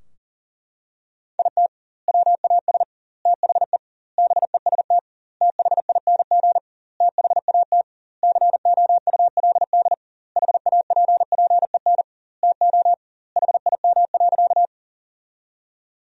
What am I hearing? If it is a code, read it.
Morse code, heard as a keyed tone, about 37 wpm, 700 Hz. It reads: IT WAS THE BEST THING THAT COULD HAPPEN TO HIM.
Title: 2x Word-Spacing - Sentences from Top 400 Words - 40wpm